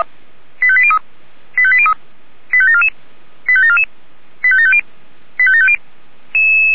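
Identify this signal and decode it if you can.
Five-tone selective call (ZVEI), heard as quick runs of five tones, tones of about 70 ms. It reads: 85803 85803 85744 85744 85750 85750
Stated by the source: ZVEI five-tone selective call received by SDR.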